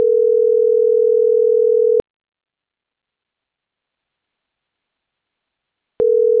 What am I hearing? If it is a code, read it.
Ringback tone (North America), 440 + 480 Hz.